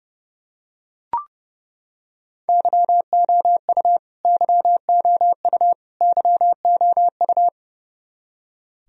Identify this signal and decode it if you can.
Morse code, heard as a keyed tone, about 30 wpm, 700 Hz. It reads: YOU YOU YOU
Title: Top 100 Words Repeated 30wpm